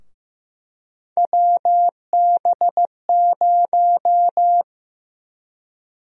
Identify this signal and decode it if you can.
Morse code, heard as a keyed tone, about 15 wpm, 700 Hz. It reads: WB0
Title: USA Call Sign Prefixes - Repeated 2x - 15wpm